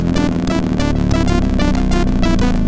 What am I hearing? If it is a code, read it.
There is no code.